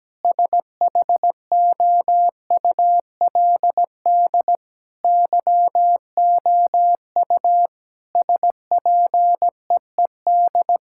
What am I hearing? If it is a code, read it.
Morse code, heard as a keyed tone, about 17 wpm, 700 Hz. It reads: SHOULD YOU SPEED
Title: Sentences from Top 1000 Words - Review Part 5 - 17wpm